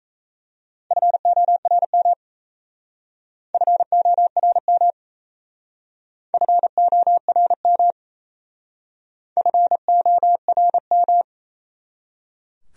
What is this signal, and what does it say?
Morse code, heard as a keyed tone, about 35 wpm, 700 Hz. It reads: FORM FORM FORM FORM